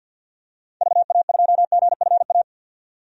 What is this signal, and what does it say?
Morse code, heard as a keyed tone, about 52 wpm, 700 Hz. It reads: VA2GFA
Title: DXCC-100 Call Signs - 6 Characters - 50wpm